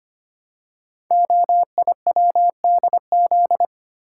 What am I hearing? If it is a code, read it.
Morse code, heard as a keyed tone, about 25 wpm, 700 Hz. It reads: OIWDZ
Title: Code Groups - 5 Letters - 25wpm